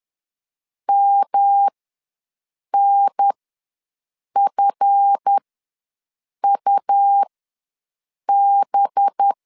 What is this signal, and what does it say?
Morse code, heard as a keyed tone, about 10 wpm, spaced as for 6 wpm, 800 Hz. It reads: MNFUB
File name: LEZIONI / 10 SIGNS / 10D COMA ws 10wpm 4314